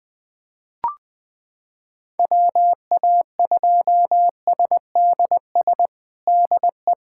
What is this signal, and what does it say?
Morse code, heard as a keyed tone, about 20 wpm, 700 Hz. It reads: WA2SDS DE
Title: USA Callsign DE Callsign - 6 Characters - 20wpm